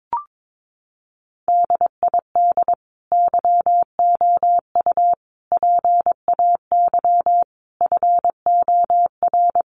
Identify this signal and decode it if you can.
Morse code, heard as a keyed tone, about 22 wpm, 700 Hz. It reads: DID YOU PAY FOR
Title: Sentences from Top 1500 Words 22wpm